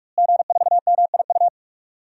Morse code, heard as a keyed tone, about 46 wpm, 700 Hz. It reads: G4MIU
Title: DXCC-100 Call Signs - 5 Characters - Repeated 2x - 45wpm